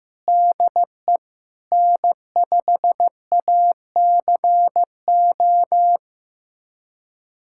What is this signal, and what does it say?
Morse code, heard as a keyed tone, about 15 wpm, 700 Hz. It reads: DE N5ACO
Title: USA Callsign DE Callsign - 5 Characters - 15wpm